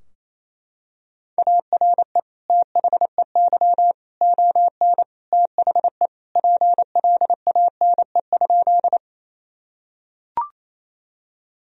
Morse code, heard as a keyed tone, about 28 wpm, 700 Hz. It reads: ARE THEY ON THE PLANE?
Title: Sentences from Top 1000 Words - Review Part 3 - 28wpm